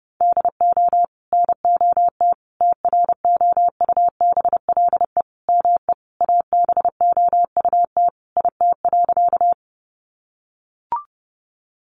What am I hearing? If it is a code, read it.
Morse code, heard as a keyed tone, about 30 wpm, 700 Hz. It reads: DO NOT TROUBLE ME ABOUT IT.
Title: Sentences from Top 1200 Words 30wpm